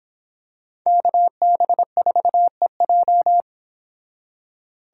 Morse code, heard as a keyed tone, about 26 wpm, 700 Hz. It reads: KB4EJ